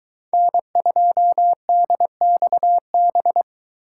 Morse code, heard as a keyed tone, about 22 wpm, 700 Hz. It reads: N2DXB